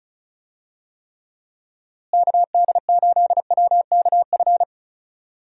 Morse code, heard as a keyed tone, about 36 wpm, 700 Hz. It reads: KD8WKF
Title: USA Call Signs - 6 Characters - Repeated 3x - 35wpm